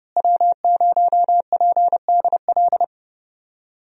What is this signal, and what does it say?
Morse code, heard as a keyed tone, about 30 wpm, 700 Hz. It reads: W0PDL
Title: USA Call Signs - 5 Characters - Repeated 3x - 30wpm